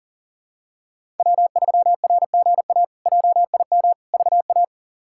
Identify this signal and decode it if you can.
Morse code, heard as a keyed tone, about 40 wpm, 700 Hz. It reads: W3RGA JIM VA
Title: K1USN SST QSO Practice - 40wpm